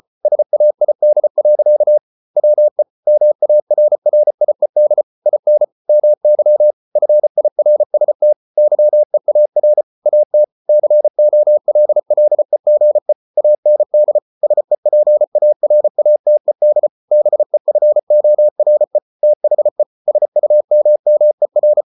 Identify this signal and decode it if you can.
Morse code, heard as a keyed tone, about 34 wpm, 600 Hz. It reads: SAID. WE MARRIED IN MY FIRST YEAR AT COLLEGE AND SEPARATED BEFORE THE SUMMER